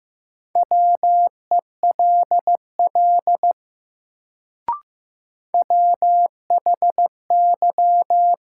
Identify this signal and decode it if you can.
Morse code, heard as a keyed tone, about 15 wpm, 700 Hz. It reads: WELL WHY
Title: Sentences from Top 200 Words - New Word Drill - 15wpm